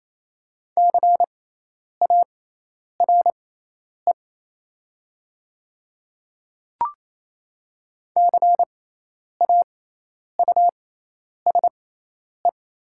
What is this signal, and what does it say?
Morse code, heard as a keyed tone, about 28 wpm, spaced as for 10 wpm, 690 Hz. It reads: CARE CAUSE
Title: 6x Character Spacing - Top 300 Words - 28wpm